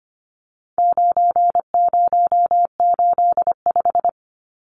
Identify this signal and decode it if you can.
Morse code, heard as a keyed tone, about 25 wpm, 700 Hz. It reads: 9085
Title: Numbers - 4-Digit 25wpm